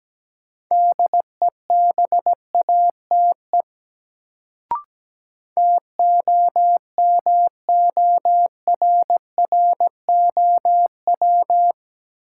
Morse code, heard as a keyed tone, about 17 wpm, 700 Hz. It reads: DEBATE TOMORROW
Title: Sentences from Top 1100 Words - New Word Drill - 17wpm